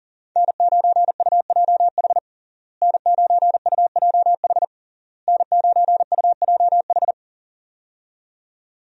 Morse code, heard as a keyed tone, about 40 wpm, 700 Hz. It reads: N9UJH N9UJH N9UJH
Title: USA Call Signs - 5 Characters - Repeated 3x - 40wpm